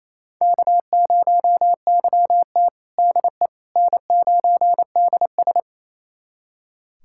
Morse code, heard as a keyed tone, about 28 wpm, 700 Hz. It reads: K0YT DE N9DS
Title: USA Callsign DE Callsign - 4 Characters - 28wpm